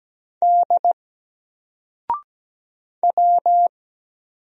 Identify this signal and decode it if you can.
Morse code, heard as a keyed tone, about 17 wpm, 700 Hz. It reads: D W
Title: Single Letter-Number 17wpm